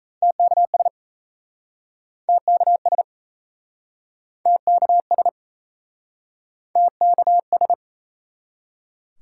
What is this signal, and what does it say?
Morse code, heard as a keyed tone, about 41 wpm, 700 Hz. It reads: TKS TKS TKS TKS